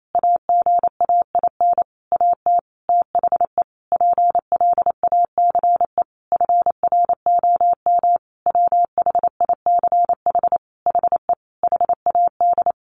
Morse code, heard as a keyed tone, about 28 wpm, 700 Hz. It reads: AGAIN AT THE PLACE FROM WHICH HE HAD
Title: Sentences from Top 1500 Words 28wpm